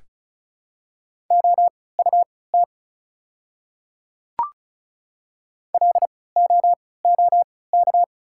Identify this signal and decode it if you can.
Morse code, heard as a keyed tone, about 36 wpm, spaced as for 20 wpm, 690 Hz. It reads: OUT LOOK
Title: 3x Character Spacing - Top 100 Words - 35wpm